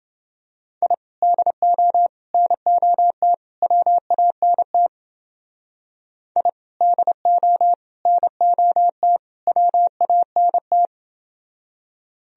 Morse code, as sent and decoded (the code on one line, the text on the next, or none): .. / -.. --- / -. --- - / .-- .- -. - / .. / -.. --- / -. --- - / .-- .- -. -
I DO NOT WANT I DO NOT WANT